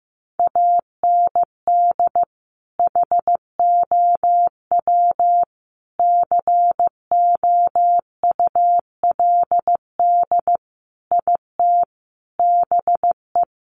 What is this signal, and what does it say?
Morse code, heard as a keyed tone, about 15 wpm, 700 Hz. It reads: AND HOW COULD IT BE
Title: Sentences from Top 1200 Words 15wpm